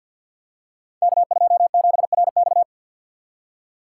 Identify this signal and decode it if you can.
Morse code, heard as a keyed tone, about 52 wpm, 700 Hz. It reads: K27RX
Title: Code Groups - 5 Letter:Numbers - 50wpm